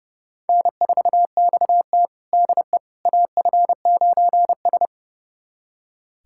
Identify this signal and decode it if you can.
Morse code, heard as a keyed tone, about 30 wpm, 700 Hz. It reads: N4XT DE AF9S
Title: USA Callsign DE Callsign - 4 Characters - 30wpm